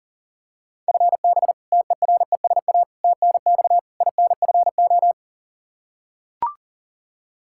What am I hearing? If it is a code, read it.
Morse code, heard as a keyed tone, about 40 wpm, 700 Hz. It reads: FB TERESA TNX INFO
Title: Practice QSO - Exchange 3 - More Info - 40wpm